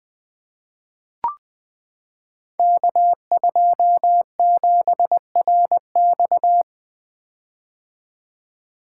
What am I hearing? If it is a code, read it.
Morse code, heard as a keyed tone, about 21 wpm, 700 Hz. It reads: K27RX
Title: Code Groups - 5 Letter:Numbers - 20wpm